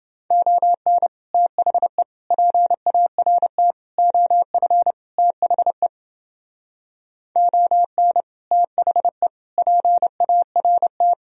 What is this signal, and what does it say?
Morse code, heard as a keyed tone, about 31 wpm, 700 Hz. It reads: ON THE PART OF THE ON THE PART